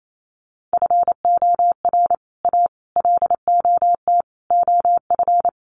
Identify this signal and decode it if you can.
Morse code, heard as a keyed tone, about 28 wpm, 700 Hz. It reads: FOR A LOT OF
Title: Sets of 5 Words Encore 28wpm